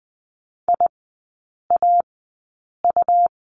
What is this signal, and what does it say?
Morse code, heard as a keyed tone, about 20 wpm, spaced as for 8 wpm, 690 Hz. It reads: IAU
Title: 159 Lesson 26e - Letter g - Three Characters 20wpm